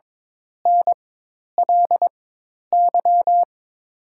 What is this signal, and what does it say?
Morse code, heard as a keyed tone, about 22 wpm, spaced as for 10 wpm, 700 Hz. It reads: NLY
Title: 4x Character Spacing - Top 200 Words - 22wpm